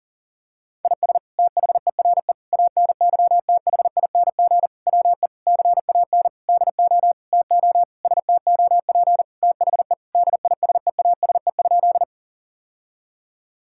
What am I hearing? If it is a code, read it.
Morse code, heard as a keyed tone, about 37 wpm, 700 Hz. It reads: IS THERE ANYTHING WE CAN DO TO STOP THE DISEASE?
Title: Sentences from Top 1000 Words - Review Part 6 - 40wpm